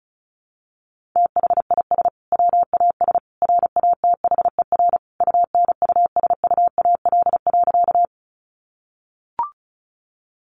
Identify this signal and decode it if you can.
Morse code, heard as a keyed tone, about 34 wpm, 700 Hz. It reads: THIS WAS RATHER UNUSUAL.
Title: Sentences from Top 1400 Words 35wpm